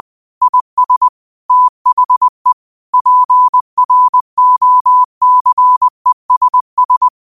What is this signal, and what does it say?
Morse code, heard as a keyed tone, about 20 wpm, 1000 Hz. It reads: IS THE PROCESS